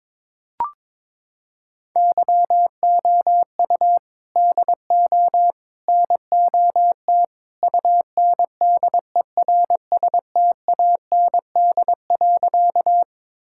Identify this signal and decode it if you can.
Morse code, heard as a keyed tone, about 22 wpm, 700 Hz. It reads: YOU DO NOT UNDERSTAND.
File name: practice / 22wpm / Sentences with 3-Word Phrases - 22wpm